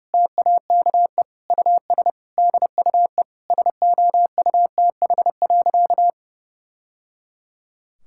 Morse code, heard as a keyed tone, about 30 wpm, 700 Hz. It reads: TAKE US DUE SOUTH.
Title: Sentences from Top 1000 Words - Review Part 6 - 30wpm